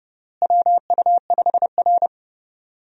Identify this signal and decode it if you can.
Morse code, heard as a keyed tone, about 32 wpm, 700 Hz. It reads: WU5R
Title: USA Call Signs - 4 Characters - Repeated 3x - 30wpm